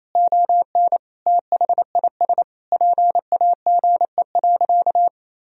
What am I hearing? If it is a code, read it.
Morse code, heard as a keyed tone, about 28 wpm, 700 Hz. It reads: ON THIS PAGE.
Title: Sentences from Top 1000 Words - Review Part 1 - 28wpm